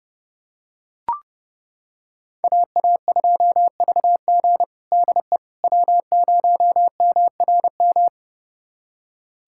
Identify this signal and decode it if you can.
Morse code, heard as a keyed tone, about 32 wpm, 700 Hz. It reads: AA2VG DE W0MRM
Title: USA Callsign DE Callsign - 5 Characters - 30wpm